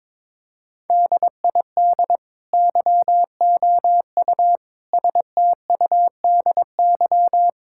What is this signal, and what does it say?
Morse code, heard as a keyed tone, about 22 wpm, 700 Hz. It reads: DID YOU STUDY
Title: Sentences from Top 1000 Words - Review Part 6 - 22wpm